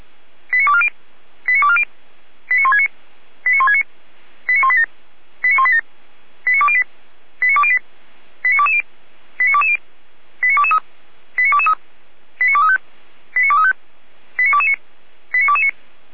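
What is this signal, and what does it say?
Five-tone selective call (ZVEI), heard as quick runs of five tones, tones of about 70 ms. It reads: 89240 89240 89169 89169 89187 89187 89208 89208 89220 89220 89223 89223 89235 89235 89229 89229
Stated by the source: ZVEI five-tone selective call received on an SDR by fire brigade Orlamünde (feuerwehr-orlamuende).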